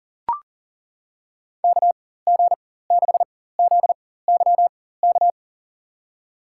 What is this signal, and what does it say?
Morse code, heard as a keyed tone, about 40 wpm, spaced as for 19 wpm, 700 Hz. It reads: KG6ZYK
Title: USA Call Signs - 4x Character Spacing - 40wpm